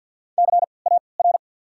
Morse code, heard as a keyed tone, about 53 wpm, spaced as for 29 wpm, 690 Hz. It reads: CAR
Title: 3x Character Spacing - Top 300 Words - 50wpm